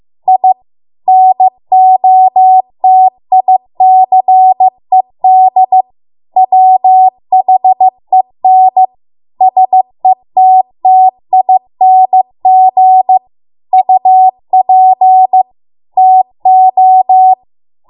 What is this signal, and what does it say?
Morse code, heard as a keyed tone, about 15 wpm, 750 Hz. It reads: I NOTICED WHEN SETTING UP TO